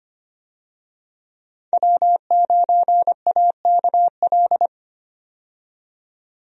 Morse code, heard as a keyed tone, about 25 wpm, 700 Hz. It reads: W9AKL
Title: USA Call Signs - 5 Characters - 25wpm